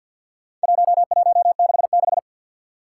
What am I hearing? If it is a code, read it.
Morse code, heard as a keyed tone, about 47 wpm, 700 Hz. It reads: 1166
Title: Four Challenging Letter-Number Combinations 50wpm